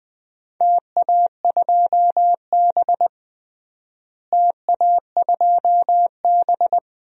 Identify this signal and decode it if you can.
Morse code, heard as a keyed tone, about 20 wpm, 700 Hz. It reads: TA2B TA2B
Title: DXCC-100 Call Signs - 4 Characters - Repeated 2x - 20wpm